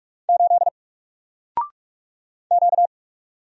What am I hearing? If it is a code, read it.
Morse code, heard as a keyed tone, about 46 wpm, 700 Hz. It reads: 8 Q
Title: Single Letter-Number Alt 45wpm